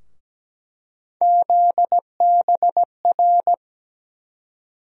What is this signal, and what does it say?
Morse code, heard as a keyed tone, about 17 wpm, 700 Hz. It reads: ZBR
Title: USA Call Sign Postfixes - 3 Letters 17wpm